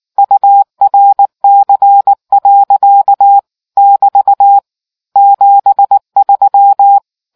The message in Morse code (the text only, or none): URC. <BT> 73